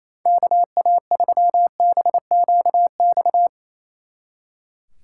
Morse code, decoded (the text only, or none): KA3BQX